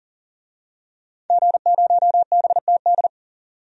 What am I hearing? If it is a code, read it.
Morse code, heard as a keyed tone, about 40 wpm, 700 Hz. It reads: G0BTD
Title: DXCC-100 Call Signs - 5 Characters - 40wpm